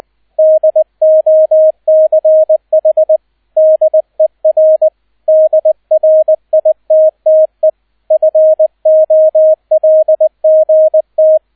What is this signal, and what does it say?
Morse code, heard as a keyed tone, about 19 wpm, 630 Hz, with radio noise. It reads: DOCH DER DRITTE FOLGT
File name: 002 Zweiter Streich 20wpm